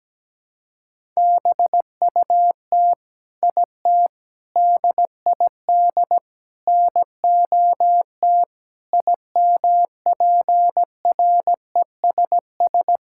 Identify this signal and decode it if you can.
Morse code, heard as a keyed tone, about 17 wpm, 700 Hz. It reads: BUT IT DID NOT IMPRESS
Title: Sentences from Top 1600 Words 17wpm